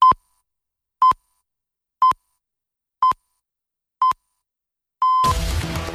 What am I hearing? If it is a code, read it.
Radio time signal (time pips). It is heard from 0.0 to 5.3 s.
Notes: Noticia radio informativa con corte de voz, sintonía y señales horarias